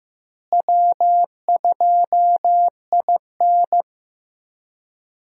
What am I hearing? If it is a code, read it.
Morse code, heard as a keyed tone, about 15 wpm, 700 Hz. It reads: W2IN
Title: USA Call Signs - 4 Characters - Repeated 3x - 15wpm